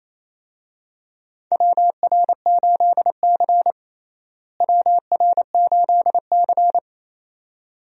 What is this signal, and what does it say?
Morse code, heard as a keyed tone, about 28 wpm, 700 Hz. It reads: WR8C WR8C
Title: USA Call Signs - 4 Characters - Repeated 2x - 28wpm